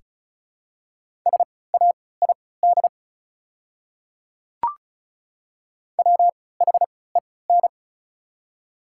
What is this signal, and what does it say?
Morse code, heard as a keyed tone, about 36 wpm, spaced as for 20 wpm, 700 Hz. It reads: SAID WHEN
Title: 3x Character Spacing - Top 100 Words - 35wpm